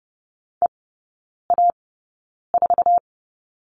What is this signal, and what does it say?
Morse code, heard as a keyed tone, about 30 wpm, spaced as for 9 wpm, 700 Hz. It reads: EA4
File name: 081 Lesson 14e - Number 2 - Three Characters 30wpm